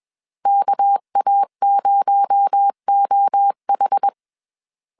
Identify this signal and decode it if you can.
Morse code, heard as a keyed tone, about 21 wpm, 800 Hz. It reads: KA0OH
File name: CW / DET / 1 6 1500C S B 20wpm 2048